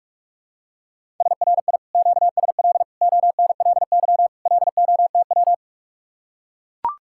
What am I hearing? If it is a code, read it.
Morse code, heard as a keyed tone, about 44 wpm, 700 Hz. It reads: SRI QSL ONLY LOTW